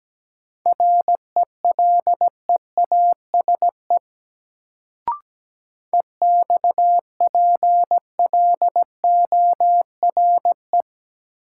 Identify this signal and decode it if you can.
Morse code, heard as a keyed tone, about 17 wpm, 700 Hz. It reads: RELEASE EXPLORE
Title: Sentences from Top 1100 Words - New Word Drill Alt - 17wpm